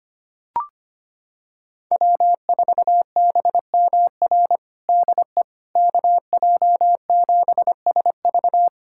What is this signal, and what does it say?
Morse code, heard as a keyed tone, about 25 wpm, 700 Hz. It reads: W4BMR DE KJ7SV
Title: USA Callsign DE Callsign - 5 Characters - 25wpm